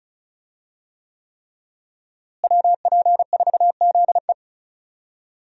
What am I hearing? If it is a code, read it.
Morse code, heard as a keyed tone, about 36 wpm, 700 Hz. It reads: WP4ZE
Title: USA Call Signs - 5 Characters - Repeated 3x - 35wpm